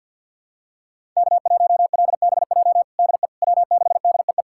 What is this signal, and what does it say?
Morse code, heard as a keyed tone, about 46 wpm, 700 Hz. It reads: K1LBJ DE W6DEE
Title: USA Callsign DE Callsign - 5 Characters - 50wpm